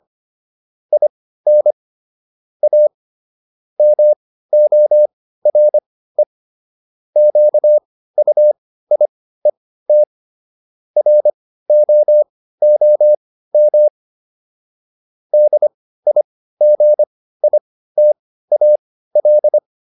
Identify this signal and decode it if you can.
Morse code, heard as a keyed tone, about 25 wpm, spaced as for 15 wpm, 600 Hz. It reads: IN A MORE QUIET ROOM DIGITAL